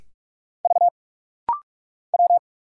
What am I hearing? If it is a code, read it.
Morse code, heard as a keyed tone, about 44 wpm, 700 Hz. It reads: V W
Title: Single Letters - Rapid-Fire 45wpm